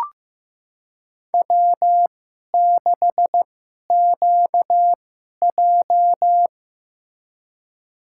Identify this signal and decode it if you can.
Morse code, heard as a keyed tone, about 15 wpm, spaced as for 11 wpm, 700 Hz. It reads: W6QJ
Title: USA Call Signs - 4 Characters - 2x Character Spacing - 15wpm